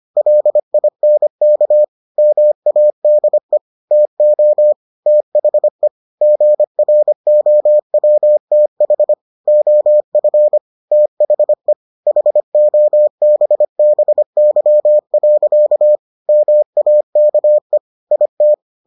Morse code, heard as a keyed tone, about 25 wpm, 600 Hz. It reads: LINK MADE TO THE GROWTH OF THE HOBBY. MAKE IT